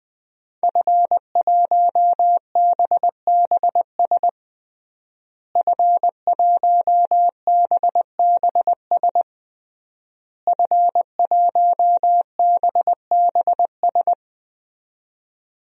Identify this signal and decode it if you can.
Morse code, heard as a keyed tone, about 20 wpm, 700 Hz. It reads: F1BBS F1BBS F1BBS
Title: DXCC-100 Call Signs - 5 Characters - Repeated 3x - 20wpm